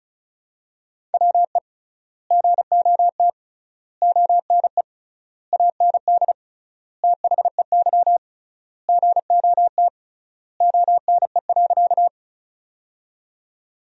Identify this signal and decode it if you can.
Morse code, heard as a keyed tone, about 36 wpm, 700 Hz. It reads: WE GOT ONE AND THEY GOT ONE.
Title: 3x Word-Spacing - Sentences from Top 300 Words - 35wpm